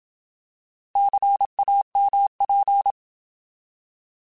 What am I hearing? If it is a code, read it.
Morse code, heard as a keyed tone, about 27 wpm, 800 Hz. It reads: CAMP
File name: Alphabetical